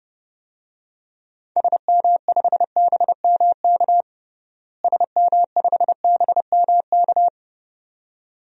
Morse code, heard as a keyed tone, about 30 wpm, 700 Hz. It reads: SM5BMK SM5BMK
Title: DXCC-100 Call Signs - 6 Characters - Repeated 2x - 30wpm